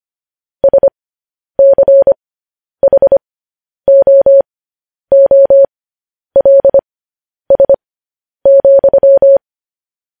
Morse code, heard as a keyed tone, about 25 wpm, spaced as for 10 wpm, 550 Hz. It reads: SCHOOLS,
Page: news headlines in morse code at 10 wpm